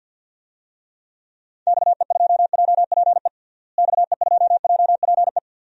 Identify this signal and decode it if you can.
Morse code, heard as a keyed tone, about 51 wpm, 700 Hz. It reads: XE2JPE XE2JPE
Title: DXCC-100 Call Signs - 6 Characters - Repeated 2x - 50wpm